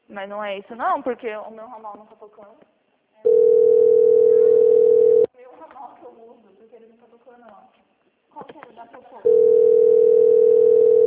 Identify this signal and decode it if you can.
Ringback tone (North America), 440 + 480 Hz, with speech.
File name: out- -2003- - - .4464